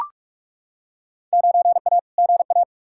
Morse code, heard as a keyed tone, about 41 wpm, 700 Hz. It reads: 9A GA
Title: Field Day Exchanges Encore - 45wpm